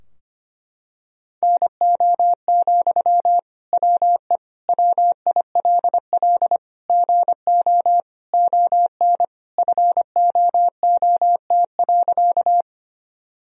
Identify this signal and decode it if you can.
Morse code, heard as a keyed tone, about 25 wpm, 700 Hz. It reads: NO, WE WILL GO ON FOOT.
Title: Sentences from Top 1000 Words - Review Part 3 - 25wpm